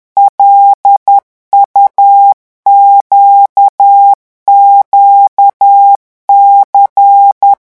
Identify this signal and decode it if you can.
Morse code, heard as a keyed tone, about 10 wpm, 800 Hz. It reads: LUQQC
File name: morse mp3 / revision5-vitesse 10 mots